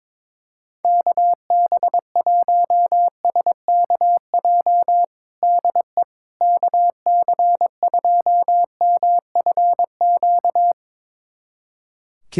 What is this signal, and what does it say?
Morse code, heard as a keyed tone, about 21 wpm, 700 Hz. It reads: KB1SKJ DE KC2MFQ
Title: USA Callsign DE Callsign 22wpm